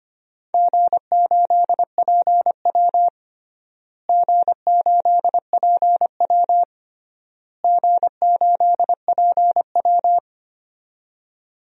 Morse code, heard as a keyed tone, about 25 wpm, 700 Hz. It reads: G8PW G8PW G8PW
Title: DXCC-100 Call Signs - 4 Characters - Repeated 3x - 25wpm